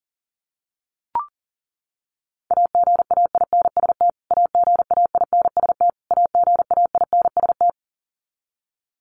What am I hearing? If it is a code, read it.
Morse code, heard as a keyed tone, about 37 wpm, 700 Hz. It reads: AGAINST AGAINST AGAINST
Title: Top 500 Words Repeated 40wpm